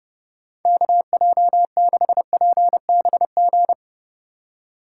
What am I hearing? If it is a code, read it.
Morse code, heard as a keyed tone, about 32 wpm, 700 Hz. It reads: KJ6PBG
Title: USA Call Signs - 3x Character Spacing - 30wpm